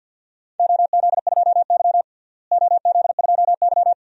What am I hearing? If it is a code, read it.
Morse code, heard as a keyed tone, about 46 wpm, 700 Hz. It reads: OZ2Y OZ2Y
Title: DXCC-100 Call Signs - 4 Characters - Repeated 2x - 50wpm